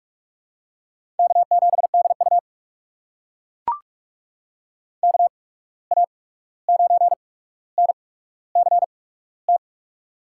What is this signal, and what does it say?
Morse code, heard as a keyed tone, about 41 wpm, spaced as for 16 wpm, 700 Hz. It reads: K7DU KA9NCT
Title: USA Call Signs - 8x Character Spacing - 45wpm